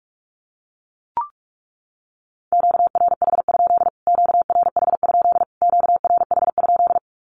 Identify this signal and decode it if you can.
Morse code, heard as a keyed tone, about 44 wpm, 700 Hz. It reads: QRH? QRH? QRH?
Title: Q-Codes Repeated 3x 45wpm